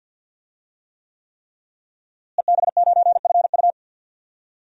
Morse code, heard as a keyed tone, about 48 wpm, 700 Hz. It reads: EB9FU
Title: DXCC-100 Call Signs - 5 Characters - Repeated 3x - 50wpm